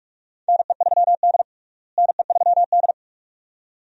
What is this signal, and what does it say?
Morse code, heard as a keyed tone, about 47 wpm, 700 Hz. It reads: NE3D NE3D
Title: USA Call Signs - 4 Characters - Repeated 2x - 45wpm